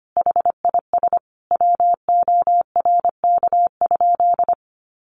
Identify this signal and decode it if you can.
Morse code, heard as a keyed tone, about 25 wpm, 700 Hz. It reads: HIS WORK?
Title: Sentences from Top 1300 Words 25wpm